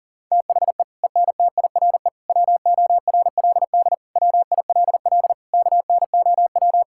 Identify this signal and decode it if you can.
Morse code, heard as a keyed tone, about 37 wpm, 700 Hz. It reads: THE ENTIRE WORLD WILL KNOW
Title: Sentences from Top 1000 Words - Review Part 5 - 40wpm